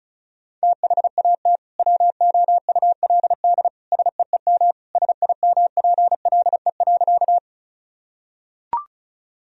Morse code, heard as a keyed tone, about 36 wpm, 700 Hz. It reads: THAT WOULD SEEM SIMPLE.
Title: Sentences from Top 1000 Words - Review Part 1 - 35wpm